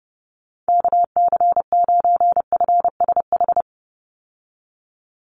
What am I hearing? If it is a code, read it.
Morse code, heard as a keyed tone, about 30 wpm, 700 Hz. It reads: KC9FSH
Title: USA Call Signs 30wpm